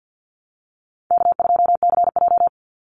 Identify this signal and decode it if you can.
Morse code, heard as a keyed tone, about 50 wpm, 700 Hz. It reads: K2CJ
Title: Call Signs - 50wpm